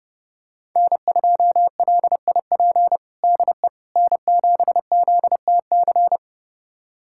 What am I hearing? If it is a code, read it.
Morse code, heard as a keyed tone, about 29 wpm, 700 Hz. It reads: N2LIP DE N7ZTC
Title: USA Callsign DE Callsign - 5 Characters - 30wpm